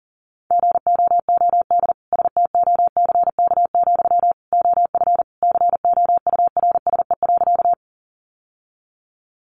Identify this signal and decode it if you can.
Morse code, heard as a keyed tone, about 37 wpm, 700 Hz. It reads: GOOD STOCK, OF COURSE.
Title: Sentences from Top 800 Words 40wpm